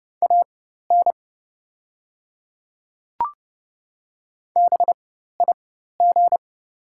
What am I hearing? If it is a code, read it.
Morse code, heard as a keyed tone, about 30 wpm, spaced as for 14 wpm, 690 Hz. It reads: AN BIG
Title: 4x Character Spacing - Top 500 Words - 30wpm